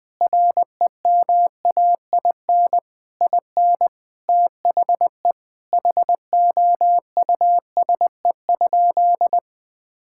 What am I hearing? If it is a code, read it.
Morse code, heard as a keyed tone, about 20 wpm, 700 Hz. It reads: REMAIN IN THE HOUSE?
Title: Sentences from Top 1600 Words 20wpm